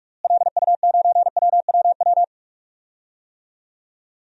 Morse code, heard as a keyed tone, about 46 wpm, 700 Hz. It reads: LU9WWW
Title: DXCC-100 Call Signs - 6 Characters - Repeated 2x - 45wpm